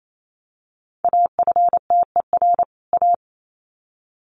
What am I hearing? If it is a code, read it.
Morse code, heard as a keyed tone, about 28 wpm, 700 Hz. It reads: AFTER A
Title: Sets of 2 Words 28wpm